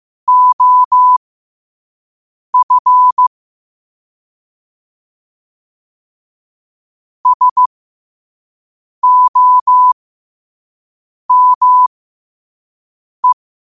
Morse code, heard as a keyed tone, about 15 wpm, spaced as for 5 wpm, 1000 Hz. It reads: OF SOME
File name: qotdfiles / 5wpm